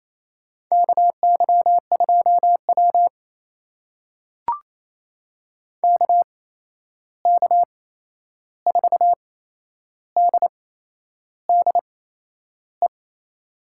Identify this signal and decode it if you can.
Morse code, heard as a keyed tone, about 28 wpm, spaced as for 11 wpm, 700 Hz. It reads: KY2W KK4DDE
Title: USA Call Signs - 8x Character Spacing - 28wpm